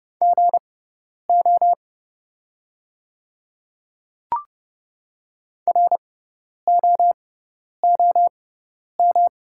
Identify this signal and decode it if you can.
Morse code, heard as a keyed tone, about 31 wpm, spaced as for 10 wpm, 700 Hz. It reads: GO ROOM
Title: 6x Character Spacing - Top 300 Words - 30wpm